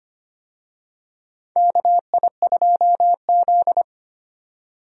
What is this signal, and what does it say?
Morse code, heard as a keyed tone, about 25 wpm, 700 Hz. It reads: KI2Z
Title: USA Call Signs - 4 Characters - 25wpm